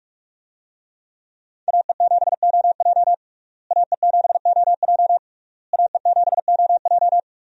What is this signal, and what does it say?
Morse code, heard as a keyed tone, about 45 wpm, 700 Hz. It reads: AE7OJ AE7OJ AE7OJ
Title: USA Call Signs - 5 Characters - Repeated 3x - 45wpm